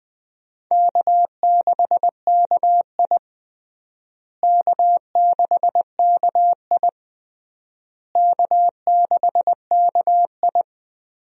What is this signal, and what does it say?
Morse code, heard as a keyed tone, about 20 wpm, 700 Hz. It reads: K6KI K6KI K6KI
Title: USA Call Signs - 4 Characters - Repeated 3x - 20wpm